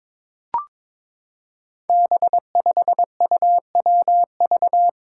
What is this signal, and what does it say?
Morse code, heard as a keyed tone, about 22 wpm, 700 Hz. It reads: B5UWV
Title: Five Challenging Letter-Number Combinations 22wpm